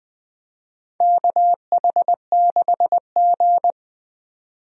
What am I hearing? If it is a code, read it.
Morse code, heard as a keyed tone, about 20 wpm, 700 Hz. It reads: KH6G
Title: DXCC-100 Call Signs - 4 Characters - 20wpm